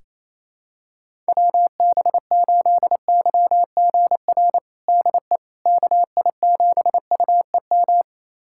Morse code, heard as a keyed tone, about 28 wpm, 700 Hz. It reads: WB8YGR DE KI7UEM
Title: USA Callsign DE Callsign - 6 Characters - 28wpm